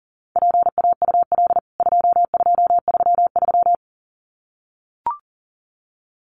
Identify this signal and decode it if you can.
Morse code, heard as a keyed tone, about 40 wpm, 700 Hz. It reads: PAUL 2233
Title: CWT Exchange 40wpm